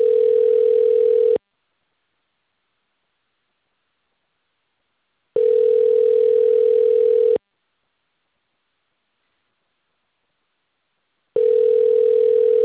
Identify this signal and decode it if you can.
Ringback tone (North America), 440 + 480 Hz.